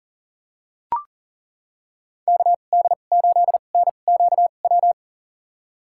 Morse code, heard as a keyed tone, about 40 wpm, spaced as for 29 wpm, 700 Hz. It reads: KD8NQW
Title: USA Call Signs - 6 Characters - 2x Character Spacing - 40wpm